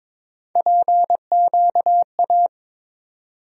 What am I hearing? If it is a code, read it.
Morse code, heard as a keyed tone, about 22 wpm, 700 Hz. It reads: PQA
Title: USA Call Sign Postfixes - 3 Letters 22wpm